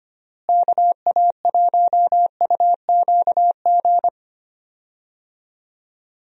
Morse code, heard as a keyed tone, about 25 wpm, 700 Hz. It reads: KA1UQG
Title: USA Call Signs - 6 Characters - Repeated 3x - 25wpm